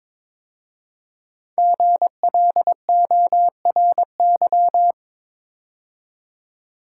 Morse code, heard as a keyed tone, about 22 wpm, 700 Hz. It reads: GLORY
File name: 5 Letter Words - First 3 Letters Given - 22wpm